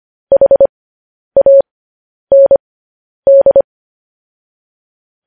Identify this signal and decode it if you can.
Morse code, heard as a keyed tone, about 25 wpm, spaced as for 10 wpm, 550 Hz. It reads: HAND
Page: news headlines in morse code at 10 wpm